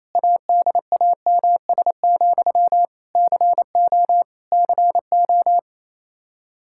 Morse code, heard as a keyed tone, about 28 wpm, 700 Hz. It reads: ADAMS, CO CO
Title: K1USN SST QSO Practice 28wpm